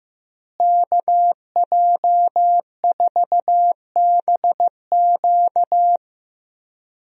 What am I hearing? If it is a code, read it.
Morse code, heard as a keyed tone, about 15 wpm, 700 Hz. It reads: KJ4BQ
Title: USA Call Signs - 5 Characters - Repeated 2x - 15wpm